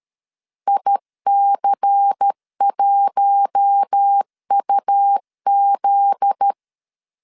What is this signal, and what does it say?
Morse code, heard as a keyed tone, about 13 wpm, 800 Hz. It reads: IC1UZ
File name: LEZIONI / 7 GF9Z / 7 GF9Z 12wpm 2147